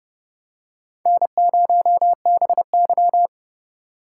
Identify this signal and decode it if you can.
Morse code, heard as a keyed tone, about 30 wpm, 700 Hz. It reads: N0BY
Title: USA Call Signs - 4 Characters - 2x Character Spacing - 30wpm